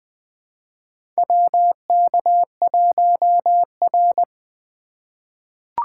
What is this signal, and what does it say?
Morse code, heard as a keyed tone, about 20 wpm, 700 Hz. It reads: WK1R
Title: USA Call Signs - 3x Character Spacing - 20wpm